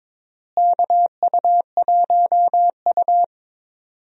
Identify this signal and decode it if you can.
Morse code, heard as a keyed tone, about 22 wpm, 700 Hz. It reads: KU1U
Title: USA Call Signs - 4 Characters - Repeated 2x - 22wpm